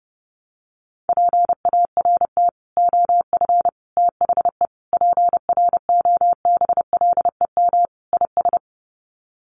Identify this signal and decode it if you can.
Morse code, heard as a keyed tone, about 30 wpm, 700 Hz. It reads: PART OF THE PROBLEM IS
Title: Sets of 5 Words Encore2 30wpm